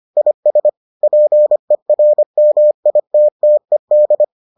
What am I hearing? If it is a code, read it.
Morse code, heard as a keyed tone, about 25 wpm, 600 Hz. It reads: IS PERMITTED